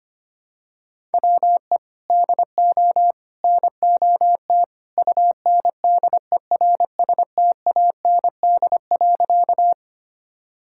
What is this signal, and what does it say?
Morse code, heard as a keyed tone, about 25 wpm, 700 Hz. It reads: WE DO NOT UNDERSTAND.
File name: practice / 25wpm / Sentences with 3-Word Phrases - 25wpm